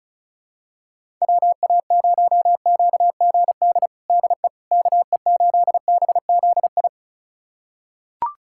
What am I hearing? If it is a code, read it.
Morse code, heard as a keyed tone, about 36 wpm, 700 Hz. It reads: WA0QGD DE KE8BZI
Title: USA Callsign DE Callsign 35wpm